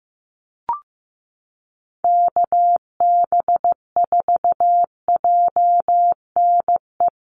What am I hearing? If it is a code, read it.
Morse code, heard as a keyed tone, about 15 wpm, 700 Hz. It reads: KB4JNE
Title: USA Call Signs Repeated 3x 15wpm